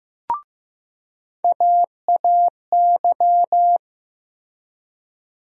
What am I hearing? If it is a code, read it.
Morse code, heard as a keyed tone, about 15 wpm, 700 Hz. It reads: AAY
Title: USA Call Sign Postfixes - 3 Letters 15wpm